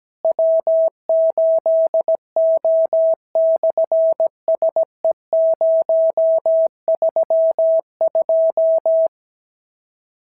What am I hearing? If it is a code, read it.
Morse code, heard as a keyed tone, about 17 wpm, 650 Hz. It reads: W8O/SE032
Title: SOTA QSO Practice 17wpm